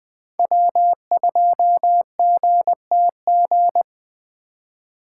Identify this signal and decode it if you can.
Morse code, heard as a keyed tone, about 24 wpm, 700 Hz. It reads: W2GTG
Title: Speed Racing - USA 1x3 Call Signs 20wpm